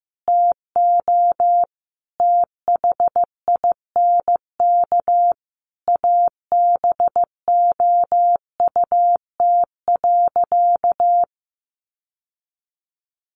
Morse code, heard as a keyed tone, about 15 wpm, 700 Hz. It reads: TO THINK ABOUT.
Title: Sentences from Top 1100 Words 15wpm